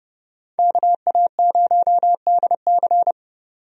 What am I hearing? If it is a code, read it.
Morse code, heard as a keyed tone, about 30 wpm, 700 Hz. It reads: KA0DC
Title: USA Call Signs - 5 Characters - 30wpm